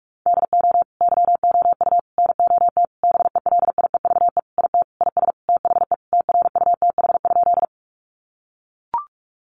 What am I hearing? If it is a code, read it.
Morse code, heard as a keyed tone, about 44 wpm, 700 Hz. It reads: DO YOU NOT BELIEVE IT IS THE TRUTH?
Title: Sentences from Top 1000 Words 45wpm